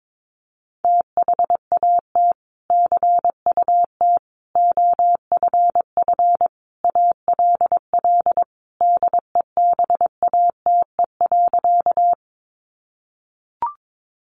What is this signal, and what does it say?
Morse code, heard as a keyed tone, about 22 wpm, 700 Hz. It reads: THAT CUT OFF ALL DEBATE.
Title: Sentences from Top 1100 Words 22wpm